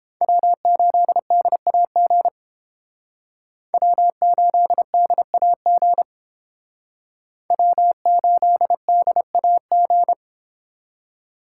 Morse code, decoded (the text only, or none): W8DAG W8DAG W8DAG